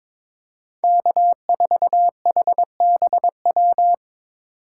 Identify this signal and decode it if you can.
Morse code, heard as a keyed tone, about 22 wpm, 700 Hz. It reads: K4HBW
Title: USA Call Signs - 5 Characters - 3x Character Spacing - 22wpm